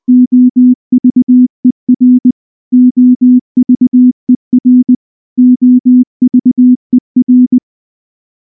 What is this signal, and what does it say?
Morse code, heard as a keyed tone, about 19 wpm, 250 Hz. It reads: OVER OVER OVER
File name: morse-training-20WPM